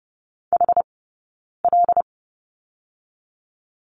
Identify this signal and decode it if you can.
Morse code, heard as a keyed tone, about 30 wpm, 700 Hz. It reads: H L
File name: course / 30:10wpm / 086 Lesson 15d - Number 5 - Two Characters 30wpm